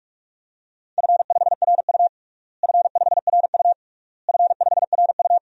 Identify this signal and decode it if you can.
Morse code, heard as a keyed tone, about 46 wpm, 690 Hz. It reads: F5RU F5RU F5RU
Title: DXCC-100 Call Signs - 4 Characters - Repeated 3x - 45wpm